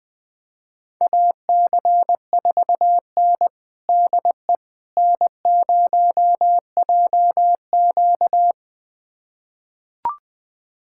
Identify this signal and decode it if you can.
Morse code, heard as a keyed tone, about 20 wpm, 700 Hz. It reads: AC4N DE N0JQ
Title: USA Callsign DE Callsign - 4 Characters - 20wpm